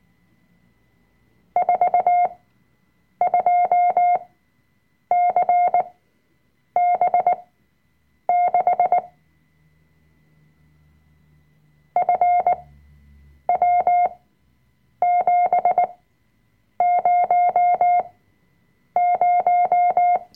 Morse code, heard as a keyed tone, about 19 wpm, spaced as for 7 wpm, 710 Hz. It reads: 42CB6 FW700
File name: Workshop Dateien / CW-Online-Kurs / Lektion 005